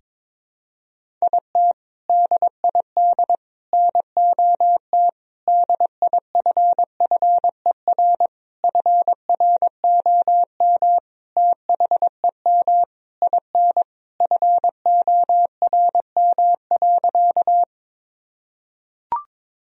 Morse code, heard as a keyed tone, about 22 wpm, 700 Hz. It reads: IT DID NOT DIFFER FROM THEM IN FORM.
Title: Sentences from Top 1000 Words - Review Part 1 - 22wpm